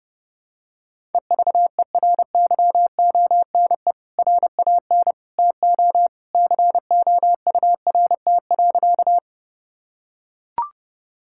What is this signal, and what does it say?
Morse code, heard as a keyed tone, about 30 wpm, 700 Hz. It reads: EVERYONE RAN TO COURT.
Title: Sentences from Top 1600 Words 30wpm